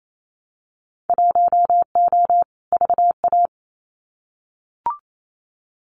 Morse code, heard as a keyed tone, about 28 wpm, 700 Hz. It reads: 1O VA
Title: Winter Field Day Exchanges 28wpm